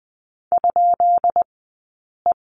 Morse code, heard as a keyed tone, about 20 wpm, spaced as for 8 wpm, 700 Hz. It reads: ?E
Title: 217 Lesson 35d - Number 0 - Two Characters 20wpm